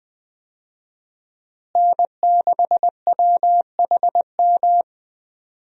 Morse code, heard as a keyed tone, about 20 wpm, 700 Hz. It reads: N6WHM
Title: USA Call Signs - 5 Characters - Repeated 2x - 20wpm